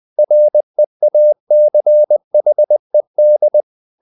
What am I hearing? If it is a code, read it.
Morse code, heard as a keyed tone, about 20 wpm, 600 Hz. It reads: REACHED
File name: the old man and the sea part 2 20wpm